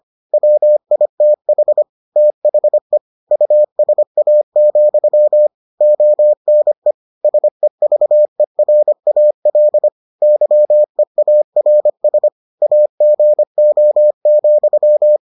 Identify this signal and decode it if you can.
Morse code, heard as a keyed tone, about 25 wpm, 600 Hz. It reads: WITH THE USA, ONE SEVERAL YEARS AGO,